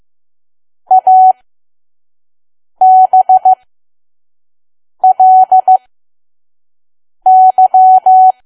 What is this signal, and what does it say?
Morse code, heard as a keyed tone, about 15 wpm, spaced as for 5 wpm, 750 Hz. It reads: ABLY
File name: Archive / 5 WPM files / PM